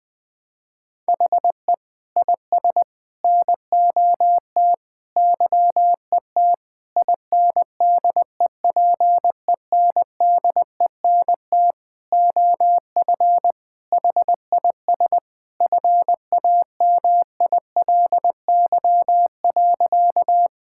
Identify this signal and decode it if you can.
Morse code, heard as a keyed tone, about 20 wpm, 700 Hz. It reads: HE IS NOT YET INDEPENDENT OF HIS FAMILY.
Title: Sentences from Top 1000 Words - Review Part 6 - 20wpm